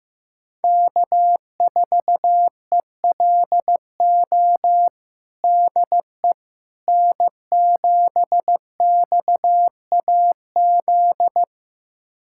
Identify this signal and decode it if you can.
Morse code, heard as a keyed tone, about 15 wpm, 700 Hz. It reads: K4ELO DE N7XAZ
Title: USA Callsign DE Callsign 15wpm